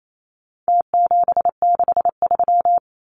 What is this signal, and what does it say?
Morse code, heard as a keyed tone, about 28 wpm, 700 Hz. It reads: T763
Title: Numbers - Contest Style - 4-Digit 28wpm